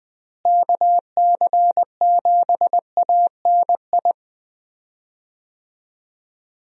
Morse code, heard as a keyed tone, about 21 wpm, 700 Hz. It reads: KC7ANI
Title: USA Call Signs - 6 Characters - Repeated 3x - 20wpm